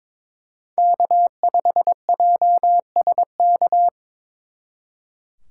Morse code, heard as a keyed tone, about 26 wpm, 700 Hz. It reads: K5JSK
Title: Speed Racing - USA 1x3 Call Signs 22wpm